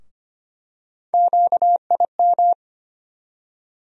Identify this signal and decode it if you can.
Morse code, heard as a keyed tone, about 25 wpm, 700 Hz. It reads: QIM
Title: USA Call Sign Postfixes - 3 Letters 25wpm